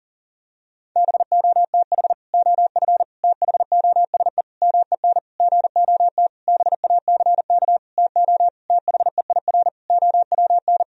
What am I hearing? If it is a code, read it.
Morse code, heard as a keyed tone, about 40 wpm, 700 Hz. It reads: BOTH OF THOSE MEN GOT BACK TO THEIR OWN
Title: Sentences from Top 1000 Words - Review Part 1 - 40wpm